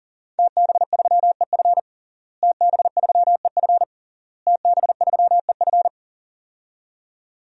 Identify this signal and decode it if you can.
Morse code, heard as a keyed tone, about 40 wpm, 700 Hz. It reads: TB3EF TB3EF TB3EF
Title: DXCC-100 Call Signs - 5 Characters - Repeated 3x - 40wpm